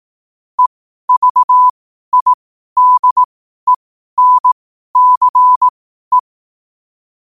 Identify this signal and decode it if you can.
Morse code, heard as a keyed tone, about 18 wpm, spaced as for 12 wpm, 1000 Hz. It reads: EVIDENCE
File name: qotdfiles / 12wpm